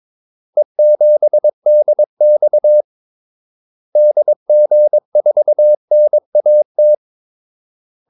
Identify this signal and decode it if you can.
Morse code, heard as a keyed tone, about 22 wpm, 600 Hz. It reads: E7DX DG4NAT
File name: Callsigns-22wpm